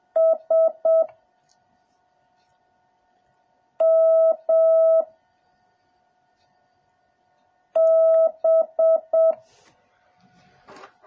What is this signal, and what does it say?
Morse code, heard as a keyed tone, about 7 wpm, 1300 Hz. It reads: S M B